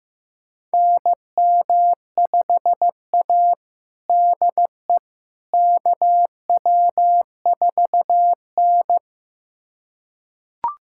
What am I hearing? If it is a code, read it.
Morse code, heard as a keyed tone, about 15 wpm, 700 Hz. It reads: NM5A DE KW4N
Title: USA Callsign DE Callsign - 4 Characters - 15wpm